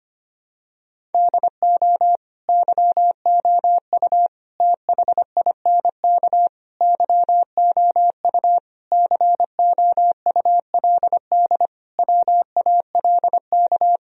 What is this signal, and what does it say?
Morse code, heard as a keyed tone, about 25 wpm, 700 Hz. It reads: DO YOU THINK YOU COULD WALK
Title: Sentences from Top 1000 Words - Review Part 1 - 25wpm